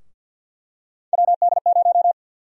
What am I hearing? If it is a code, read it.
Morse code, heard as a keyed tone, about 52 wpm, 700 Hz. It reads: WD0
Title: USA Call Sign Prefixes - Repeated 2x - 50wpm